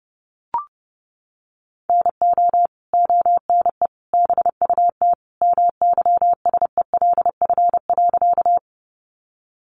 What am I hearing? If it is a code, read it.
Morse code, heard as a keyed tone, about 30 wpm, 700 Hz. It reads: NO ONE BUT MYSELF.
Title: Sentences from Top 1300 Words 30wpm